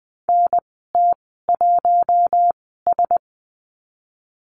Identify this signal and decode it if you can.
Morse code, heard as a keyed tone, about 20 wpm, spaced as for 15 wpm, 700 Hz. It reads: NT1S
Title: 042 Lesson 8g - Number 1 - Call Signs 20wpm